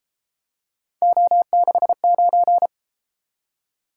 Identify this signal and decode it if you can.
Morse code, heard as a keyed tone, about 30 wpm, 700 Hz. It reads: O69